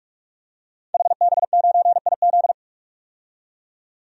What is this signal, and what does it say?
Morse code, heard as a keyed tone, about 47 wpm, 700 Hz. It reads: HB9IZ
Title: DXCC-100 Call Signs - 5 Characters - 45wpm